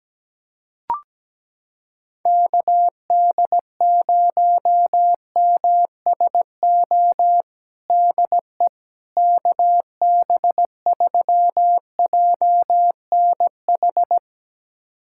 Morse code, heard as a keyed tone, about 17 wpm, 700 Hz. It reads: KD0MSO DE KB3JNH
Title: USA Callsign DE Callsign - 6 Characters - 17wpm